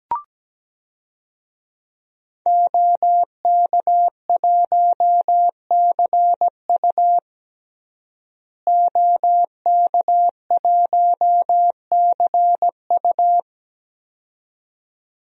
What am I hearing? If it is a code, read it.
Morse code, heard as a keyed tone, about 17 wpm, 700 Hz. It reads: OK1CU OK1CU
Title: DXCC-100 Call Signs - 5 Characters - Repeated 2x - 17wpm